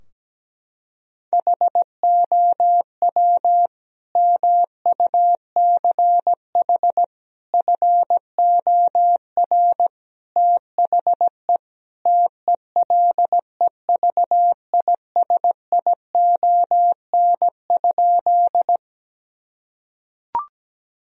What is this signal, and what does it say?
Morse code, heard as a keyed tone, about 17 wpm, 700 Hz. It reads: HOW MUCH FOR THE TELEVISION?
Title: Sentences from Top 1000 Words - Review Part 4 - 17wpm